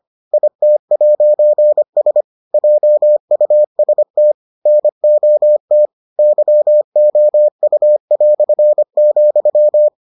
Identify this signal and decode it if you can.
Morse code, heard as a keyed tone, about 25 wpm, 600 Hz. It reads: IT'S JUST NOT YOU",